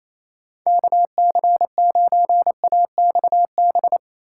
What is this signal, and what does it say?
Morse code, heard as a keyed tone, about 38 wpm, 700 Hz. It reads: KC9AXB